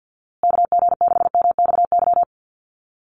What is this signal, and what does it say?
Morse code, heard as a keyed tone, about 50 wpm, 700 Hz. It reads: KG6MXY
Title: USA Call Signs Repeated 3x 50wpm